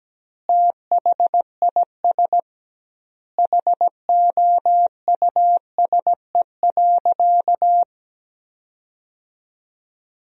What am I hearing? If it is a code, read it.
Morse code, heard as a keyed tone, about 17 wpm, 700 Hz. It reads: THIS HOUSE.
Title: 2x Word-Spacing - Sentences from Top 400 Words - 17wpm